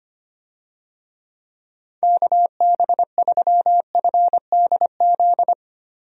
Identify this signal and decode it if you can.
Morse code, heard as a keyed tone, about 25 wpm, 700 Hz. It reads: KB3FDZ
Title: USA Call Signs - 6 Characters - 25wpm